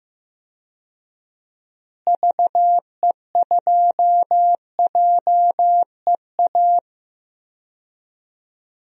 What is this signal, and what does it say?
Morse code, heard as a keyed tone, about 15 wpm, 700 Hz. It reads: VE2JEA
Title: DXCC-100 Call Signs - 6 Characters - Repeated 2x - 15wpm